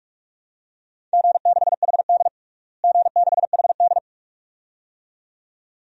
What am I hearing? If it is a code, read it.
Morse code, heard as a keyed tone, about 47 wpm, 700 Hz. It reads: G6HD G6HD
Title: DXCC-100 Call Signs - 4 Characters - Repeated 2x - 45wpm